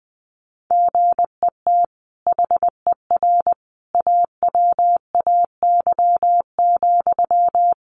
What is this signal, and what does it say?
Morse code, heard as a keyed tone, about 20 wpm, 700 Hz. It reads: GET HER AWAY,
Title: Sentences from Top 1500 Words 20wpm